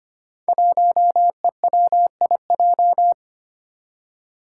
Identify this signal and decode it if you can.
Morse code, heard as a keyed tone, about 25 wpm, 700 Hz. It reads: 1EWIJ
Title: Five Challenging Letter-Number Combinations 25wpm